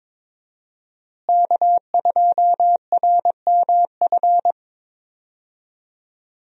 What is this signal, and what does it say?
Morse code, heard as a keyed tone, about 22 wpm, 700 Hz. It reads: K2RMF
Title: USA Call Signs - 5 Characters - Repeated 2x - 22wpm